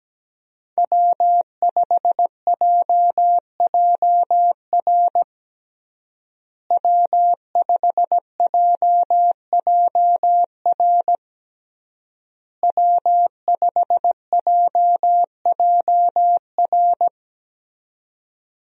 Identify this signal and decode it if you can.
Morse code, heard as a keyed tone, about 17 wpm, 700 Hz. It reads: W5JJR W5JJR W5JJR
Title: USA Call Signs - 5 Characters - Repeated 3x - 17wpm